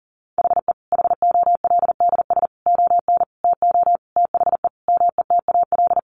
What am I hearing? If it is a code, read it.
Morse code, heard as a keyed tone, about 40 wpm, 700 Hz. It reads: HE HOLDS ON TO THE METAL
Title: Additional Easy Phrases 40wpm